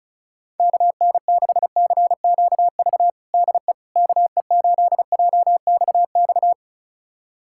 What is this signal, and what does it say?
Morse code, heard as a keyed tone, about 36 wpm, 700 Hz. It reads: KN6CQV DE KE8JXX
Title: USA Callsign DE Callsign - 6 Characters - 35wpm